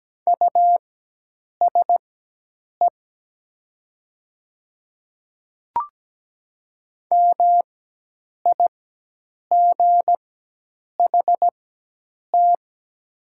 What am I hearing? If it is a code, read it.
Morse code, heard as a keyed tone, about 17 wpm, spaced as for 8 wpm, 700 Hz. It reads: USE MIGHT
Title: 4x Character Spacing - Top 300 Words - 17wpm